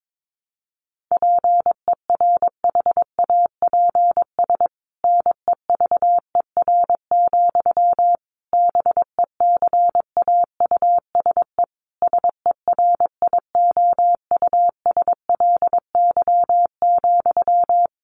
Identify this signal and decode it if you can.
Morse code, heard as a keyed tone, about 22 wpm, 700 Hz. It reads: PERHAPS NEVER, BECAUSE SERIOUSLY,
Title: Sentences from Top 1200 Words 22wpm